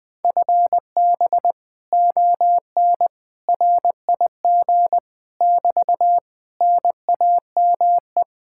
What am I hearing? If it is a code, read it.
Morse code, heard as a keyed tone, about 20 wpm, 700 Hz. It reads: FB ON RIG <BT> NAME
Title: Speed-Racing Common QSO Phrases - 17wpm